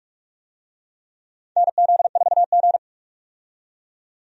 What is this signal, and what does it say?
Morse code, heard as a keyed tone, about 44 wpm, 700 Hz. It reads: NZ4G
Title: USA Call Signs - 4 Characters - 45wpm